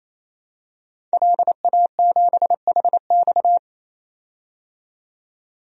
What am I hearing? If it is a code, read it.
Morse code, heard as a keyed tone, about 28 wpm, 700 Hz. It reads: LA7HX
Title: DXCC-100 Call Signs - 5 Characters - 28wpm